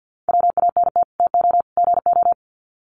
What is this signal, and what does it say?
Morse code, heard as a keyed tone, about 50 wpm, 700 Hz. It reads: WANT TO GO
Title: Sets of 3 Words 50wpm